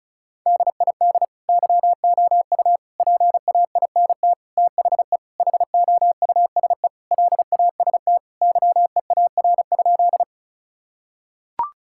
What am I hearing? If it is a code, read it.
Morse code, heard as a keyed tone, about 36 wpm, 700 Hz. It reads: DID YOU PAINT THE HOUSE LAST YEAR?
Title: Sentences from Top 1000 Words - Review Part 6 - 35wpm